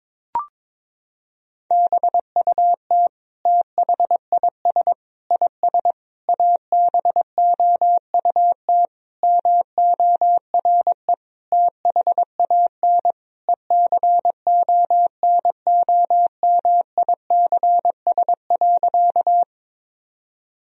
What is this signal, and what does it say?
Morse code, heard as a keyed tone, about 22 wpm, 700 Hz. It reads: BUT THIS IS ABOUT MORE THAN ECONOMICS.
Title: Sentences from Top 1000 Words - Review Part 3 - 22wpm